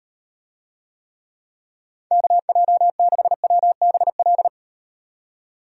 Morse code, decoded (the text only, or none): KJ6WBL